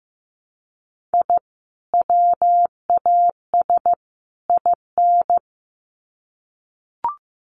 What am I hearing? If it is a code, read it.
Morse code, heard as a keyed tone, about 15 wpm, 700 Hz. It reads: I WAS IN
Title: Sets of 3 Words 15wpm